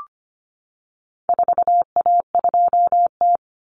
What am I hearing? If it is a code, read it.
Morse code, heard as a keyed tone, about 25 wpm, 700 Hz. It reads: 4A2T
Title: Common Cut Numbers - 4-Digit 25wpm